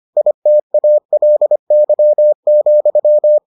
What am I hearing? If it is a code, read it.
Morse code, heard as a keyed tone, about 25 wpm, 600 Hz. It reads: ITALY,